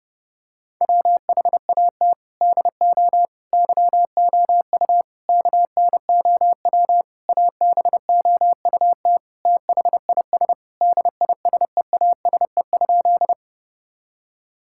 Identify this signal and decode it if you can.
Morse code, heard as a keyed tone, about 30 wpm, 700 Hz. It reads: WHAT DO YOU KNOW ABOUT THIS DISEASE?
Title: Sentences from Top 1000 Words - Review Part 6 - 30wpm